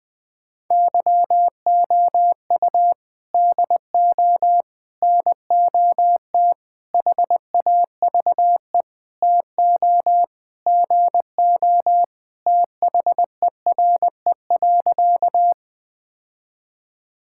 Morse code, heard as a keyed tone, about 21 wpm, 700 Hz. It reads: YOU DO NOT HAVE TO GO THERE.
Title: Sentences with 5-Word Phrases - 20wpm